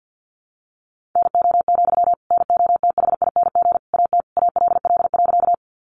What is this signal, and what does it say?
Morse code, heard as a keyed tone, about 50 wpm, 700 Hz. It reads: NO, NOTHING AT ALL.
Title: Sentences from Top 500 Words 50wpm